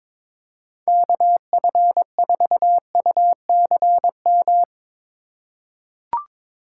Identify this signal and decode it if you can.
Morse code, heard as a keyed tone, about 22 wpm, 700 Hz. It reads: KF4UCM
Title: USA Call Signs - 6 Characters - 3x Character Spacing - 22wpm